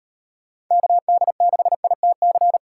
Morse code, heard as a keyed tone, about 38 wpm, 700 Hz. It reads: KD6ITC